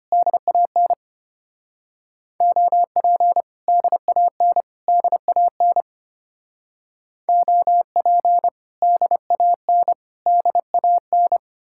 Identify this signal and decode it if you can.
Morse code, heard as a keyed tone, about 28 wpm, 700 Hz. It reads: DAN OP DAN DAN OP DAN DAN
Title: Speed-Racing Common QSO Phrases - 25wpm